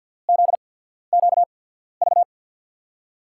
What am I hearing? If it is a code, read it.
Morse code, heard as a keyed tone, about 47 wpm, spaced as for 13 wpm, 700 Hz. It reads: ZQV
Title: USA Call Signs - 6 Characters - 8x Character Spacing - 50wpm